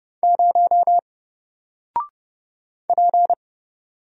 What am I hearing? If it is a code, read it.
Morse code, heard as a keyed tone, about 30 wpm, 700 Hz. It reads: 0 P
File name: Single-Letter-Number-30wpm